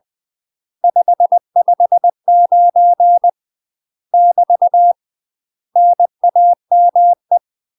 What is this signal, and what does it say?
Morse code, heard as a keyed tone, about 20 wpm, 700 Hz. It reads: 559 <BT> NAME
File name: UK MASTER QSOs TRAINING / 20WPM / UK MASTER QSOs 0070